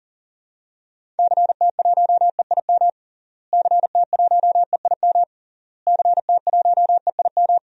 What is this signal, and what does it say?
Morse code, heard as a keyed tone, about 37 wpm, 700 Hz. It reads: CT1EIM CT1EIM CT1EIM
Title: DXCC-100 Call Signs - 6 Characters - Repeated 3x - 40wpm